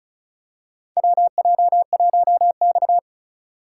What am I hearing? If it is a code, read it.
Morse code, heard as a keyed tone, about 34 wpm, 700 Hz. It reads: WJ1X
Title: USA Call Signs - 4 Characters - 2x Character Spacing - 35wpm